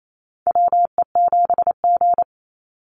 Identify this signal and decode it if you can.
Morse code, heard as a keyed tone, about 28 wpm, 700 Hz. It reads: WE7G
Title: Call Signs Repeated 3x - 28wpm